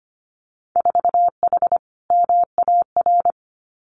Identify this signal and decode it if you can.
Morse code, heard as a keyed tone, about 25 wpm, 700 Hz. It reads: 4H MAR
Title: Winter Field Day Exchanges Encore 25wpm